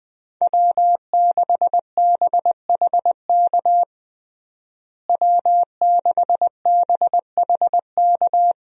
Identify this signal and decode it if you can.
Morse code, heard as a keyed tone, about 20 wpm, 700 Hz. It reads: W6BHK W6BHK
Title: USA Call Signs - 5 Characters - Repeated 2x - 20wpm